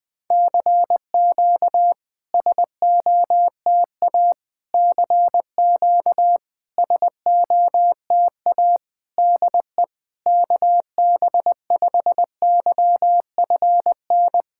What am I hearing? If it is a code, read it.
Morse code, heard as a keyed tone, about 20 wpm, 690 Hz. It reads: CQ SOTA CQ SOTA DE KB5YFN
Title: SOTA QSO Practice 20wpm